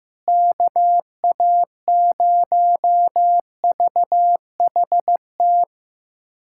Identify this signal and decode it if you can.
Morse code, heard as a keyed tone, about 15 wpm, 700 Hz. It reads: KA0VHT
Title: USA Callsign DE Callsign - 6 Characters - 15wpm